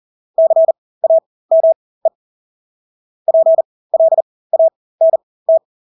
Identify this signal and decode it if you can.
Morse code, heard as a keyed tone, about 40 wpm, spaced as for 20 wpm, 650 Hz. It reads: CAME PLANT
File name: Engelske ord 40-20 wpm